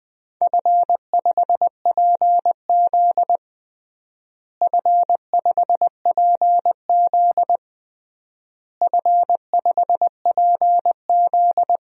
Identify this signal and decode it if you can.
Morse code, heard as a keyed tone, about 20 wpm, 700 Hz. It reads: F5PZ F5PZ F5PZ
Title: DXCC-100 Call Signs - 4 Characters - Repeated 3x - 20wpm